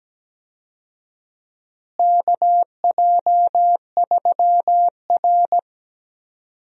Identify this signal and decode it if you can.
Morse code, heard as a keyed tone, about 17 wpm, 700 Hz. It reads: KJ3R
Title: USA Call Signs - 4 Characters - Repeated 3x - 17wpm